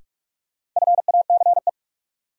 Morse code, heard as a keyed tone, about 42 wpm, 700 Hz. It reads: FACE
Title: Sets of 3 Words - New Word Drill - 45wpm